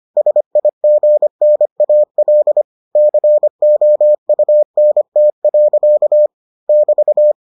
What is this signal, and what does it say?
Morse code, heard as a keyed tone, about 25 wpm, 600 Hz. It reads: SIGNAL COUNT. <BT>